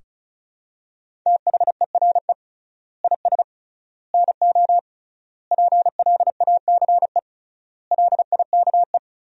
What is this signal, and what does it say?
Morse code, heard as a keyed tone, about 36 wpm, 700 Hz. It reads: THERE IS NO PLACE LIKE
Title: 3x Word-Spacing - Sentences from Top 200 Words - 35wpm